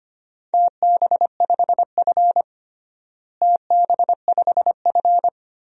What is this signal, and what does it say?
Morse code, heard as a keyed tone, about 25 wpm, 700 Hz. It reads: TB5F TB5F
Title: DXCC-100 Call Signs - 4 Characters - Repeated 3x - 25wpm